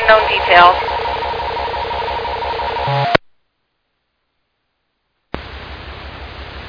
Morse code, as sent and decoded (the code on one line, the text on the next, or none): none